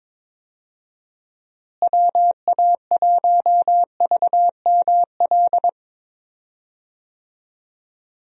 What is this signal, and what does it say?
Morse code, heard as a keyed tone, about 22 wpm, 700 Hz. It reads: WA1VML
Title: USA Call Signs - 6 Characters - Repeated 3x - 22wpm